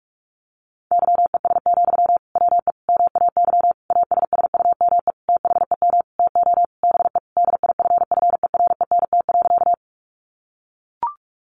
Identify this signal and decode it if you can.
Morse code, heard as a keyed tone, about 44 wpm, 700 Hz. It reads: YES, WE MAY ASSUME THEM TO BE DIFFERENT.
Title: Sentences from Top 1500 Words 45wpm